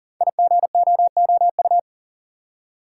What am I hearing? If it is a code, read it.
Morse code, heard as a keyed tone, about 38 wpm, 690 Hz. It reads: IGOOU
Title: Code Groups - 5 Letters - 40wpm